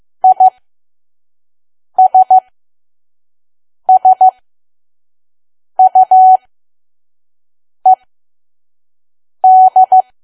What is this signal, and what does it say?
Morse code, heard as a keyed tone, about 15 wpm, spaced as for 5 wpm, 750 Hz. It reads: ISSUED
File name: Archive / 5 WPM files / PM